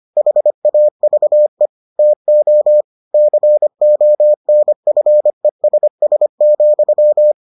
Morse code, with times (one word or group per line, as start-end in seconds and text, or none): 0.2-1.7 HAVE
2.0-2.8 TO
3.1-7.3 CONFESS,